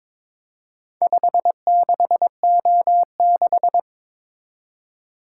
Morse code, heard as a keyed tone, about 22 wpm, 700 Hz. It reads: 56O6
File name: practice / 22wpm / Four Letter-Number Combinations - 22wpm